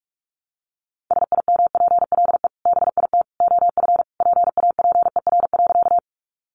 Morse code, heard as a keyed tone, about 44 wpm, 700 Hz. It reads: SIMPLE BIT OF PAPER.
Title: Sentences from Top 900 Words 45wpm